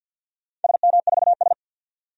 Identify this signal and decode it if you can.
Morse code, heard as a keyed tone, about 46 wpm, 700 Hz. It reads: SM4S
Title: DXCC-100 Call Signs - 4 Characters - 50wpm